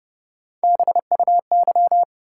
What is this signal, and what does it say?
Morse code, heard as a keyed tone, about 29 wpm, 700 Hz. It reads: BUY
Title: Sentences from Top 700 Words - New Word Drill - 30wpm